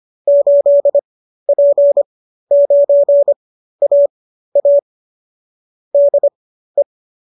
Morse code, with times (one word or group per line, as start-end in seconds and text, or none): 0.3-4.8 8P9AA
5.9-6.8 DE